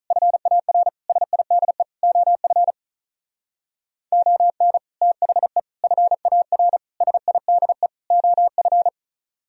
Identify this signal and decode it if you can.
Morse code, heard as a keyed tone, about 44 wpm, 700 Hz. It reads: FAR SIDE OF ON THE FAR SIDE OF
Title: Speed-Racing - Sets of 5 Words 35wpm